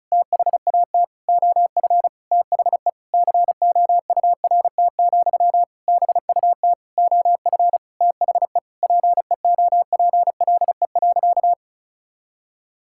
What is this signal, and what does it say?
Morse code, heard as a keyed tone, about 36 wpm, 700 Hz. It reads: THAT OF THE COURT, BUT OF THE PEOPLE.
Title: Sentences from Top 1600 Words 35wpm